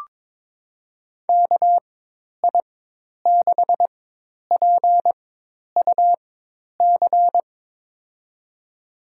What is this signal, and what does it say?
Morse code, heard as a keyed tone, about 22 wpm, spaced as for 10 wpm, 700 Hz. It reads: KI6PUC
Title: USA Call Signs - 6 Characters - 4x Character Spacing - 22wpm